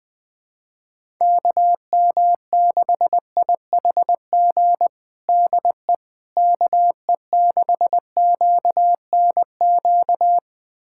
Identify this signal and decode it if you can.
Morse code, heard as a keyed tone, about 19 wpm, 700 Hz. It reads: KM6IHG DE KE6QNQ
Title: USA Callsign DE Callsign - 6 Characters - 20wpm